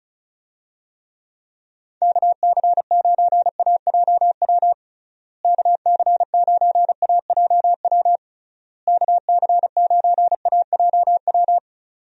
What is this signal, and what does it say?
Morse code, heard as a keyed tone, about 34 wpm, 700 Hz. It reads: KC9AJW KC9AJW KC9AJW
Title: USA Call Signs - 6 Characters - Repeated 3x - 35wpm